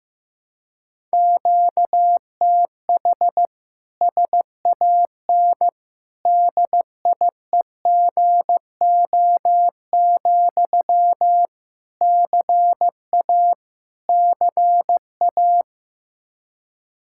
Morse code, heard as a keyed tone, about 15 wpm, 700 Hz. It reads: QTH SAN DIEGO, CA CA
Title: Practice QSO - Exchange 1 - Beginning - 15wpm